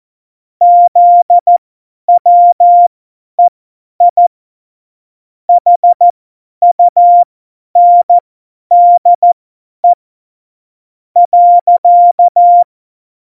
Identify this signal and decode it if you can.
Morse code, heard as a keyed tone, about 14 wpm, spaced as for 10 wpm, 700 Hz. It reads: ZWEI HUNDE .
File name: Morsekurs / Witze / Witz 12. Juni 2018, 50 F70